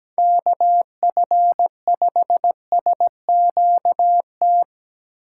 Morse code, heard as a keyed tone, about 17 wpm, 700 Hz. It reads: KF5SQT
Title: USA Call Signs - 6 Characters - 4x Character Spacing - 17wpm